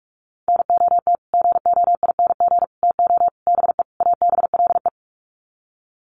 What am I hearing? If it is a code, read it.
Morse code, heard as a keyed tone, about 44 wpm, 700 Hz. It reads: NOT GOING TO BE ABLE
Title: Sets of 5 Words 45wpm